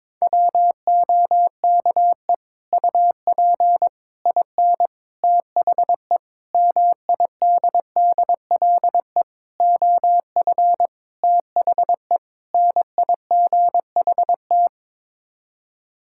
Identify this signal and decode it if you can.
Morse code, heard as a keyed tone, about 22 wpm, 700 Hz. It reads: WOKE UP IN THE MIDDLE OF THE NIGHT
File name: practice / 22wpm / Sets of 8 Words - 22wpm